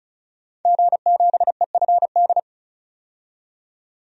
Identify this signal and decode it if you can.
Morse code, heard as a keyed tone, about 34 wpm, 700 Hz. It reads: G7EFD
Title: DXCC-100 Call Signs - 5 Characters - 35wpm